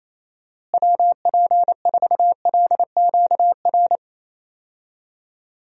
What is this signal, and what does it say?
Morse code, heard as a keyed tone, about 28 wpm, 700 Hz. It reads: WP4LQR
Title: USA Call Signs - 6 Characters - Repeated 3x - 28wpm